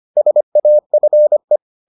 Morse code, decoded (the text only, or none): SAFE